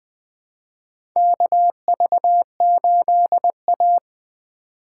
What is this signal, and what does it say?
Morse code, heard as a keyed tone, about 20 wpm, 700 Hz. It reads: KV8A
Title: USA Call Signs - 4 Characters - 2x Character Spacing - 20wpm